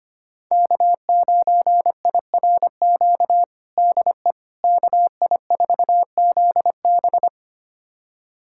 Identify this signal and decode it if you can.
Morse code, heard as a keyed tone, about 25 wpm, 700 Hz. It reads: K9IRQ DE KI4ZB
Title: USA Callsign DE Callsign - 5 Characters - 25wpm